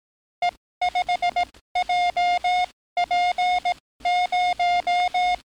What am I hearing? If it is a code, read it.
Morse code, heard as a keyed tone, about 18 wpm, spaced as for 14 wpm, 710 Hz. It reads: E5JP0
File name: training / NAVPERS 11049to11060 / NAVPERS 11057 RA RB